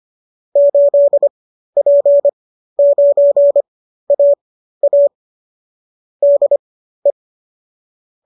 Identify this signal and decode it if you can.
Morse code, heard as a keyed tone, about 25 wpm, spaced as for 13 wpm, 590 Hz. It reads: 8P9AA DE